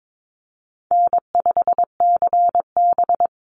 Morse code, heard as a keyed tone, about 22 wpm, 700 Hz. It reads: N5CB
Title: USA Call Signs 22wpm